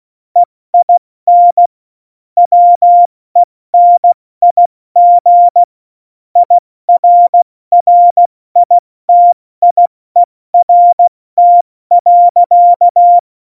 Morse code, heard as a keyed tone, about 16 wpm, 700 Hz. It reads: EIN WENIG IRRITIERT.
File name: Morsekurs / Witze / Witz 29. September 2020, 70 F80